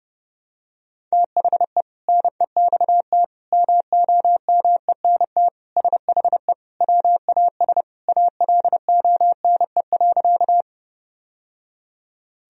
Morse code, heard as a keyed tone, about 30 wpm, 700 Hz. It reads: THE NEXT MOMENT SHE WAS ALONE.
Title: Sentences from Top 1000 Words - Review Part 1 - 30wpm